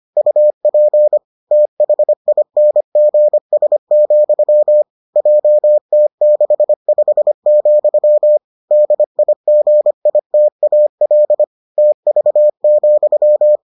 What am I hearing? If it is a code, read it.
Morse code, heard as a keyed tone, about 25 wpm, 600 Hz. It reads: UP THINGS, JT65, DIGITAL TV,